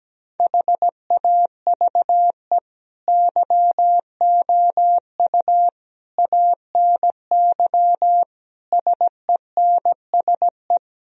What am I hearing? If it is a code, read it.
Morse code, heard as a keyed tone, about 17 wpm, 700 Hz. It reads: HAVE YOU ANY SENSE
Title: Sentences from Top 1000 Words - Review Part 4 - 17wpm